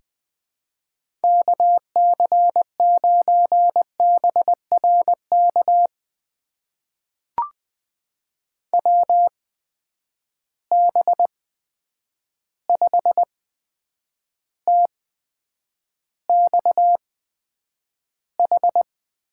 Morse code, heard as a keyed tone, about 20 wpm, spaced as for 9 wpm, 700 Hz. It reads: KC9BRK WB5TXH
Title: USA Call Signs - 6 Characters - 8x Character Spacing - 20wpm